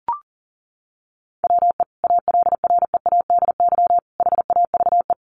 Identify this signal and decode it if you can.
Morse code, heard as a keyed tone, about 40 wpm, 700 Hz. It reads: WE ALREADY HAVE
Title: Sentences from Top 700 Words 40wpm